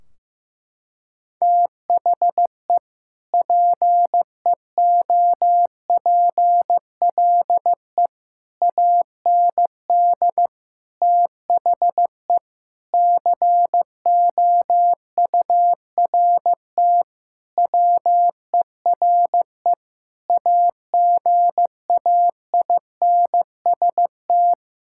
Morse code, heard as a keyed tone, about 15 wpm, 700 Hz. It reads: THE PEOPLE AND THE COURT WERE AGAINST
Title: Sentences from Top 1600 Words 15wpm